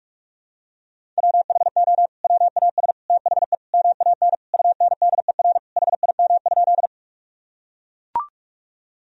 Morse code, heard as a keyed tone, about 42 wpm, 700 Hz. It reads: WHO WAS THE MAN UNDER HIM?
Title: Sentences from Top 1000 Words - Review Part 1 - 45wpm